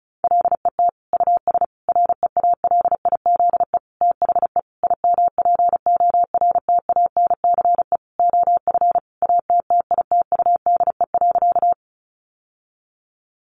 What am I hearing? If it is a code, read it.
Morse code, heard as a keyed tone, about 34 wpm, 700 Hz. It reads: LET US REALIZE THE IMPORTANCE OF ATTITUDE.
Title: Sentences from Top 1300 Words 35wpm